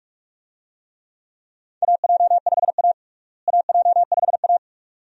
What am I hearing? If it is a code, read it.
Morse code, heard as a keyed tone, about 46 wpm, 700 Hz. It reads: AJ5A AJ5A
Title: USA Call Signs - 4 Characters - Repeated 2x - 45wpm